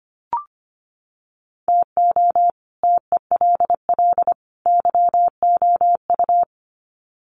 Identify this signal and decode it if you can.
Morse code, heard as a keyed tone, about 25 wpm, 700 Hz. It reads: TO TELL YOU
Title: Sets of 3 Words 25wpm